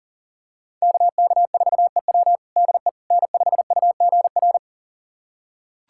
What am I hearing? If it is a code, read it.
Morse code, heard as a keyed tone, about 40 wpm, 700 Hz. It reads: KK4EW DE N5UGR
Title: USA Callsign DE Callsign - 5 Characters - 40wpm